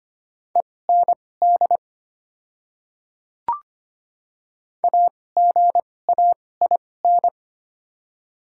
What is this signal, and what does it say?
Morse code, heard as a keyed tone, about 25 wpm, spaced as for 18 wpm, 700 Hz. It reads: END AGAIN
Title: 2x Character Spacing - Top 200 Words - 25wpm